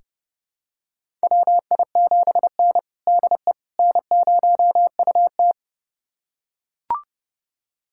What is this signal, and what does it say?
Morse code, heard as a keyed tone, about 32 wpm, 700 Hz. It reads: WI7N DE N0UT
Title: USA Callsign DE Callsign - 4 Characters - 30wpm